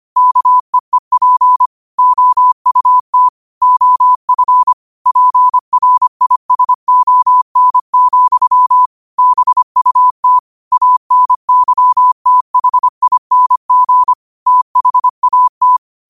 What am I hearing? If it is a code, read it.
Morse code, heard as a keyed tone, about 25 wpm, 1000 Hz. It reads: KEEP OUT OF PRISON, BUT ANYTHING THAT
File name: qotdfiles / 25wpm